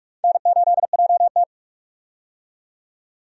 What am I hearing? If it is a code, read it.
Morse code, heard as a keyed tone, about 46 wpm, 700 Hz. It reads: N8JT
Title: USA Call Signs - 4 Characters - 45wpm